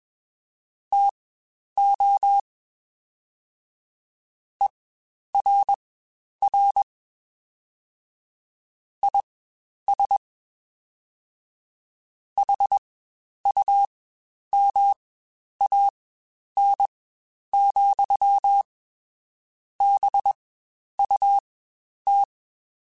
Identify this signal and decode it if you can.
Morse code, heard as a keyed tone, about 21 wpm, spaced as for 10 wpm, 800 Hz. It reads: TO ERR IS HUMAN, BUT
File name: Spaced / Chuckles Spaced-mp3s / HumanErr sp @20